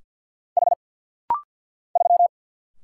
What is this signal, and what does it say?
Morse code, heard as a keyed tone, about 50 wpm, 700 Hz. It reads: H 3
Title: Single Letter-Number - Rapid-Fire 50wpm